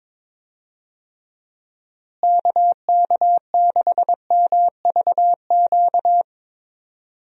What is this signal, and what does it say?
Morse code, heard as a keyed tone, about 22 wpm, 700 Hz. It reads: KK6MVQ
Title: USA Call Signs - 6 Characters - Repeated 3x - 22wpm